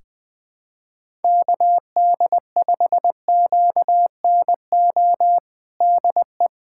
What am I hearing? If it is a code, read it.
Morse code, heard as a keyed tone, about 20 wpm, 700 Hz. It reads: KD5QNO DE
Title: USA Callsign DE Callsign 20wpm